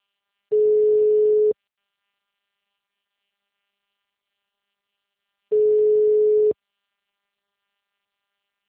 Ringback tone (Europe, 425 Hz).
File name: out- -2006- - - .303